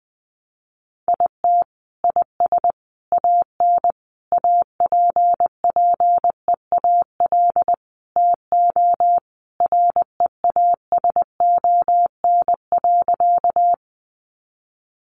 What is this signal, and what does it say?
Morse code, heard as a keyed tone, about 20 wpm, 700 Hz. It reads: IT IS AN APPEAL TO REASON.
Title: Sentences from Top 1300 Words 20wpm